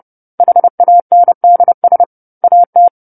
Morse code, heard as a keyed tone, about 30 wpm, 700 Hz. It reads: HANDS AT
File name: 10-Chapter-10 30 WPM